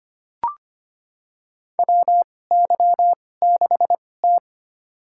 Morse code, heard as a keyed tone, about 25 wpm, spaced as for 18 wpm, 700 Hz. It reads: WY6T
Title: USA Call Signs - 4 Characters - 2x Character Spacing - 25wpm